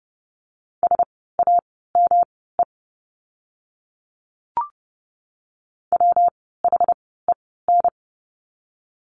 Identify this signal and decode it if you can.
Morse code, heard as a keyed tone, about 30 wpm, spaced as for 17 wpm, 700 Hz. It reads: SAME WHEN
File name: course / 30:10wpm / 115 Lesson 19f - Letter w - Words 30wpm